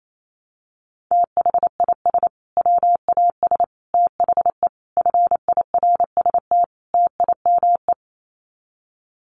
Morse code, heard as a keyed tone, about 28 wpm, 700 Hz. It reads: THIS WAS THE FIRST TIME
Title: Sets of 5 Words Encore 28wpm